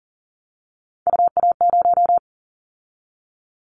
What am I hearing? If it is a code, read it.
Morse code, heard as a keyed tone, about 40 wpm, 700 Hz. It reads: UA0
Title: Callsign Prefixes for Top 100 DXCC - 40wpm